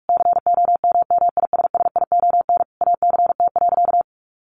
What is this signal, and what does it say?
Morse code, heard as a keyed tone, about 44 wpm, 700 Hz. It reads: COMMISSION ACT.
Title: Sentences from Top 1400 Words 45wpm